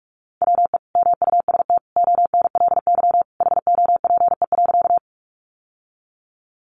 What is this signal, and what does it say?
Morse code, heard as a keyed tone, about 44 wpm, 700 Hz. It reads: WE MUST ONLY HOPE.
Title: Sentences from Top 1000 Words 45wpm